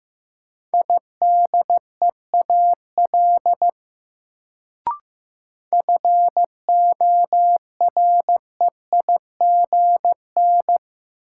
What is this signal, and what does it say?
Morse code, heard as a keyed tone, about 15 wpm, 700 Hz. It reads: IDEAL FOREIGN
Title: Sentences from Top 1400 Words - New Word Drill - 15wpm